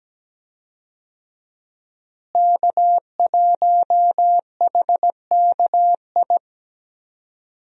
Morse code, heard as a keyed tone, about 17 wpm, 700 Hz. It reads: K1HKI
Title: USA Call Signs - 5 Characters - Repeated 2x - 17wpm